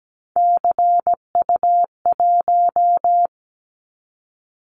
Morse code, heard as a keyed tone, about 17 wpm, 700 Hz. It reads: CU1
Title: Callsign Prefixes for Top 100 DXCC - 17wpm